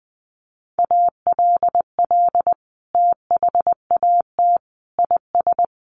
Morse code, heard as a keyed tone, about 20 wpm, 700 Hz. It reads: ALL THAT IS
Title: Sentences from Top 1100 Words 20wpm